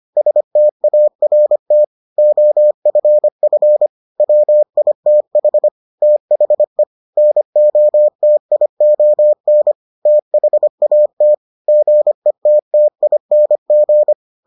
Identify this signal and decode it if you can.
Morse code, heard as a keyed tone, about 25 wpm, 600 Hz. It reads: START OFF WITH THE NOTION THAT GETTING